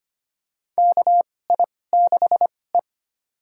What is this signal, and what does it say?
Morse code, heard as a keyed tone, about 25 wpm, spaced as for 18 wpm, 700 Hz. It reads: KI6E
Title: USA Call Signs - 4 Characters - 2x Character Spacing - 25wpm